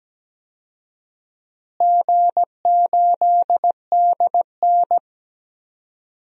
Morse code, heard as a keyed tone, about 17 wpm, 700 Hz. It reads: G8DN
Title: DXCC-100 Call Signs - 4 Characters - Repeated 3x - 17wpm